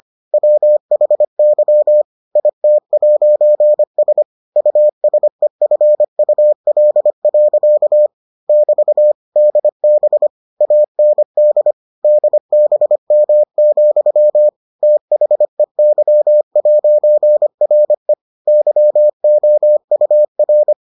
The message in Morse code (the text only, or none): WHY IT'S USEFUL. <BT> DB AND DBM, THEY'RE YOUR